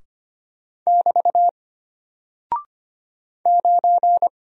Morse code, heard as a keyed tone, about 25 wpm, 700 Hz. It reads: <BT> 9
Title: Single Character (Letter:Number:Punctuation:Prosign) Alt 25wpm